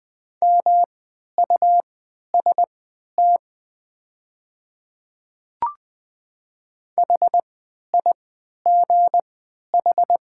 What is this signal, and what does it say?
Morse code, heard as a keyed tone, about 20 wpm, spaced as for 11 wpm, 700 Hz. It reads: MUST HIGH
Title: 3x Character Spacing - Top 200 Words - 20wpm